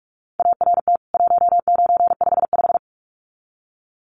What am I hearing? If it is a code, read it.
Morse code, heard as a keyed tone, about 44 wpm, 700 Hz. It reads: ART 1955
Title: CWT Exchange 45wpm